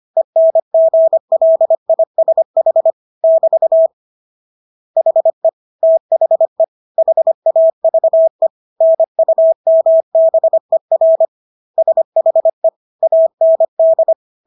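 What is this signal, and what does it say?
Morse code, heard as a keyed tone, about 25 wpm, 650 Hz. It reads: ENGLISH <BT> HE THE HAVE NUMBER SHE AND
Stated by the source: Morse practice of English words at 25 wpm from CW med Gnister.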